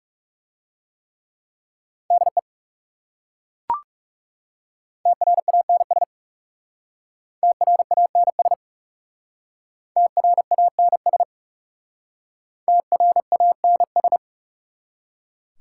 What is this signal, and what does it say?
Morse code, heard as a keyed tone, about 40 wpm, 700 Hz. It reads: DE TRANS TRANS TRANS TRANS